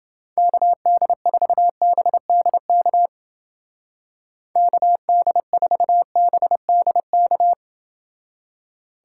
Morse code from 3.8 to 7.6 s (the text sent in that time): KD4BDK